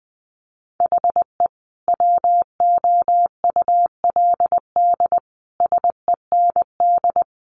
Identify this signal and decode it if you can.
Morse code, heard as a keyed tone, about 19 wpm, 700 Hz. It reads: HE WOULD SEND
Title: Sentences from Top 1500 Words 20wpm